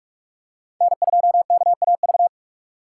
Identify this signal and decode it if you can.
Morse code, heard as a keyed tone, about 46 wpm, 700 Hz. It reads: N2KAV
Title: USA Call Signs - 5 Characters - 2x Character Spacing - 45wpm